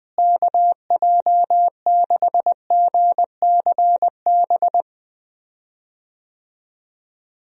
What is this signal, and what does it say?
Morse code, heard as a keyed tone, about 20 wpm, 700 Hz. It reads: KJ6GCB
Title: USA Call Signs - 6 Characters - Repeated 2x - 20wpm